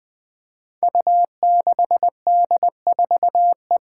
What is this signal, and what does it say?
Morse code, heard as a keyed tone, about 20 wpm, 700 Hz. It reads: U6D4E
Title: Five Challenging Letter-Number Combinations 20wpm